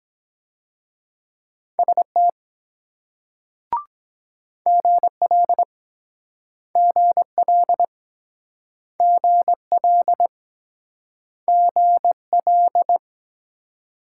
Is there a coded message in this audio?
Yes: Morse code from 1.8 to 13.0 s, heard as a keyed tone, about 22 wpm, 700 Hz. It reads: ST GL GL GL GL